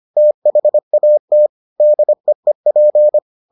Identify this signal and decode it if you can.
Morse code, heard as a keyed tone, about 25 wpm, 600 Hz. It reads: THAT DEEP